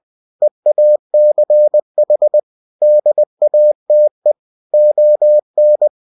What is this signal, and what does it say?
Morse code, heard as a keyed tone, about 20 wpm, 600 Hz. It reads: EACH DATE ON
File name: cw-books / winnie-the-pooh / winnie-20wpm-page-0117